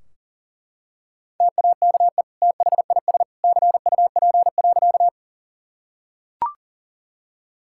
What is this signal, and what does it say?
Morse code, heard as a keyed tone, about 40 wpm, 700 Hz. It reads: TAKE THIS CUP.
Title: Sentences from Top 1600 Words 40wpm